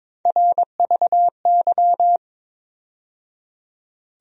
Morse code, heard as a keyed tone, about 21 wpm, 700 Hz. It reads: RVY
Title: USA Call Sign Postfixes - 3 Letters 22wpm